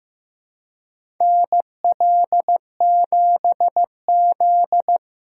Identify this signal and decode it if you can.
Morse code, heard as a keyed tone, about 15 wpm, 700 Hz. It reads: NL7Z
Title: USA Call Signs - 4 Characters - 2x Character Spacing - 15wpm